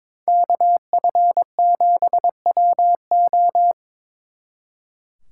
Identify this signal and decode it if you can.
Morse code, heard as a keyed tone, about 25 wpm, 700 Hz. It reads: KF7WO
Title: Speed Racing - USA 2x2 Call Signs 22wpm